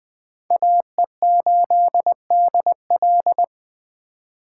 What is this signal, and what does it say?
Morse code, heard as a keyed tone, about 20 wpm, 700 Hz. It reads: AE8DL
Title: USA Call Signs - 5 Characters - 4x Character Spacing - 20wpm